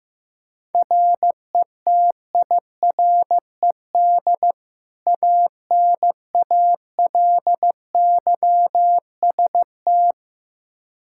Morse code, heard as a keyed tone, about 15 wpm, 700 Hz. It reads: RETIRED ANALYST
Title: K1USN SST QSO Practice 15wpm